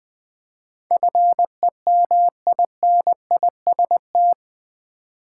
Morse code, heard as a keyed tone, about 20 wpm, 700 Hz. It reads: FEMINIST
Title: 8 Letter Words - First 3 Letters Given - 20wpm